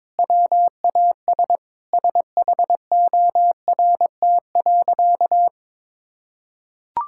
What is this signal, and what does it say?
Morse code, heard as a keyed tone, about 23 wpm, 700 Hz. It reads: WAS SHORT.
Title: Sentences from Top 1600 Words 22wpm